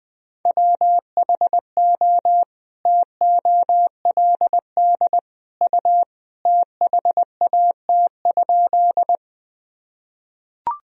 Morse code, heard as a keyed tone, about 19 wpm, 700 Hz. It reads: WHO TOLD U THAT?
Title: Sentences from Top 400 Words and Abbreviations 20wpm